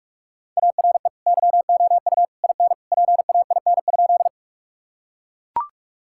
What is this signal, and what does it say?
Morse code, heard as a keyed tone, about 41 wpm, 700 Hz. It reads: ARE YOU IN PAIN?
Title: Sentences from Top 1600 Words 45wpm